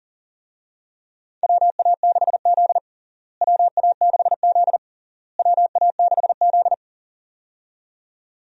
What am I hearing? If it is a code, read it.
Morse code, heard as a keyed tone, about 40 wpm, 700 Hz. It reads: WA6Z WA6Z WA6Z
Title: USA Call Signs - 4 Characters - Repeated 3x - 40wpm